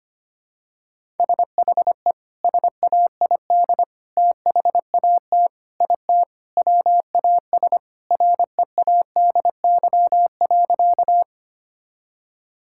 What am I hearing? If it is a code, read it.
Morse code, heard as a keyed tone, about 25 wpm, 700 Hz. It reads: SHE SAID THAT IT WAS READY.
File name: practice / 25wpm / Sentences with 3-Word Phrases - 25wpm